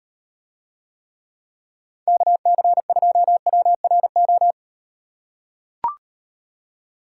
Morse code, heard as a keyed tone, about 38 wpm, 700 Hz. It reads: KC2WRO